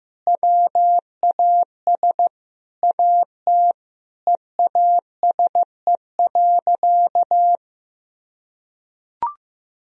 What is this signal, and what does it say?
Morse code, heard as a keyed tone, about 15 wpm, 700 Hz. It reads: WAS AT EASE.
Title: Sentences from Top 1000 Words - Review Part 1 - 15wpm